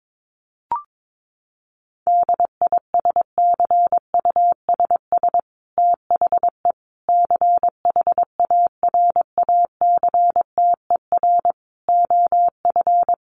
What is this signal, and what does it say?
Morse code, heard as a keyed tone, about 22 wpm, 700 Hz. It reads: DISCUSS THE CHARACTER OF
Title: Sentences from Top 1500 Words 22wpm